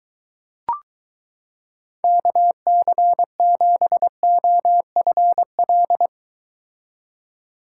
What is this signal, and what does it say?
Morse code, heard as a keyed tone, about 22 wpm, 700 Hz. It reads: KC7OFL